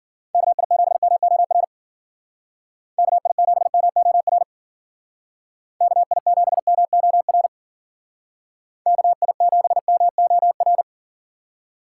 Morse code, heard as a keyed tone, about 48 wpm, 700 Hz. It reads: KI7MOR KI7MOR KI7MOR KI7MOR
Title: Speed Racing - USA 2x3 Call Signs 40wpm